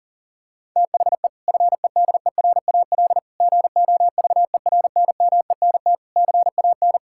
Morse code, heard as a keyed tone, about 40 wpm, 700 Hz. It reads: THE FEDERAL GOVERNMENT CAN
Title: Sentences from Top 1600 Words 40wpm